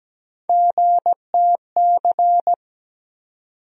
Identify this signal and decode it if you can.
Morse code, heard as a keyed tone, about 17 wpm, 700 Hz. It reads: GTC
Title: USA Call Sign Postfixes - 3 Letters 17wpm